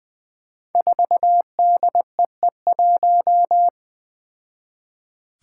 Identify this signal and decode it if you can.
Morse code, heard as a keyed tone, about 20 wpm, 700 Hz. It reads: 4DEE1
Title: Five Challenging Letter-Number Combinations 20wpm